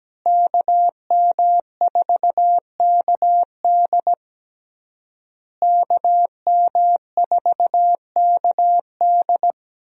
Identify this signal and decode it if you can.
Morse code, heard as a keyed tone, about 17 wpm, 700 Hz. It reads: KM4KD KM4KD
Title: USA Call Signs - 5 Characters - Repeated 3x - 17wpm